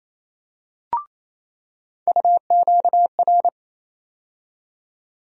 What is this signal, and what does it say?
Morse code, heard as a keyed tone, about 27 wpm, 700 Hz. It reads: UQR
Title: USA Call Sign Postfixes - 3 Letters 28wpm